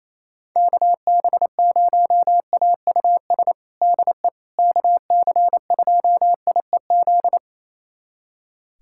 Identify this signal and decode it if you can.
Morse code, heard as a keyed tone, about 28 wpm, 700 Hz. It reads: KB0AUS DE KC2IEZ
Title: USA Callsign DE Callsign 28wpm